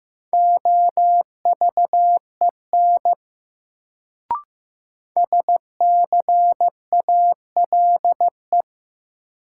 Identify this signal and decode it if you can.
Morse code, heard as a keyed tone, about 15 wpm, 700 Hz. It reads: OVEN SCALE
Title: Sentences from Top 1000 Words - Review Part 4 - New Word Drill - 15wpm